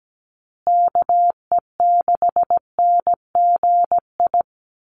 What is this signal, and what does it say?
Morse code, heard as a keyed tone, about 17 wpm, 700 Hz. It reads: KE6NGI
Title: USA Call Signs Repeated 2x 17wpm